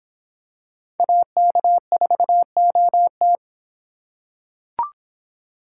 Morse code, heard as a keyed tone, about 26 wpm, 700 Hz. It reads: AK4OT